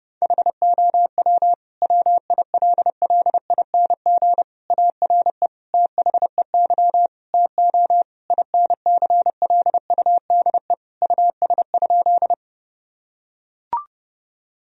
Morse code, heard as a keyed tone, about 30 wpm, 700 Hz. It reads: HOW WILLING ARE THEY TO INCLUDE US?
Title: Sentences from Top 1000 Words - Review Part 3 - 30wpm